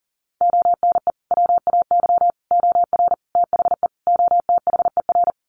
Morse code, heard as a keyed tone, about 40 wpm, 700 Hz. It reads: ONE WAY OR THE OTHER
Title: Sets of 5 Words 40wpm